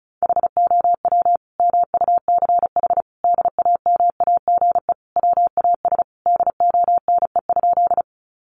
Morse code, heard as a keyed tone, about 34 wpm, 700 Hz. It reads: HOW MUCH DAMAGE WAS DONE?
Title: Sentences from Top 1100 Words 35wpm